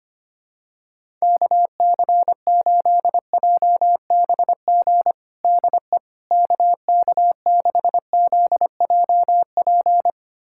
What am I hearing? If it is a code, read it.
Morse code, heard as a keyed tone, about 25 wpm, 700 Hz. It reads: KC8JBG DE KK6ZJP
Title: USA Callsign DE Callsign - 6 Characters - 25wpm